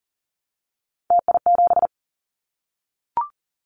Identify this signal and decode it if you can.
Morse code, heard as a keyed tone, about 40 wpm, 700 Hz. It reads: TI7
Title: Callsign Prefixes for Top 100 DXCC - 40wpm